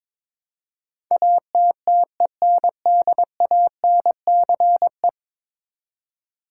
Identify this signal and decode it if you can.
Morse code, heard as a keyed tone, about 21 wpm, 700 Hz. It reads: ATTENDANCE
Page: morse-code-ninja-practice-22wpm